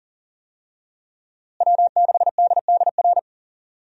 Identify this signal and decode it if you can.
Morse code, heard as a keyed tone, about 40 wpm, 700 Hz. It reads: W6DDR
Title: USA Call Signs - 5 Characters - Repeated 3x - 40wpm